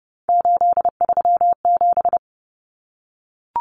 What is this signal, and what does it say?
Morse code, heard as a keyed tone, about 30 wpm, 700 Hz. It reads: 837
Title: Numbers - 3-Digit 30wpm